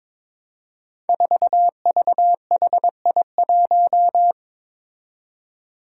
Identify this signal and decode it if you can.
Morse code, heard as a keyed tone, about 21 wpm, 700 Hz. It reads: 4VHI1
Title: Five Challenging Letter-Number Combinations 22wpm